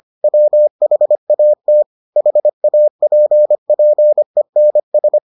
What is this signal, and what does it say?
Morse code, heard as a keyed tone, about 25 wpm, 600 Hz. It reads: WHAT HAPPENS